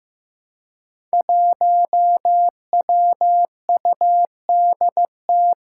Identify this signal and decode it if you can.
Morse code, heard as a keyed tone, about 15 wpm, 700 Hz. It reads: 1WUDT
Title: Five Challenging Letter-Number Combinations 15wpm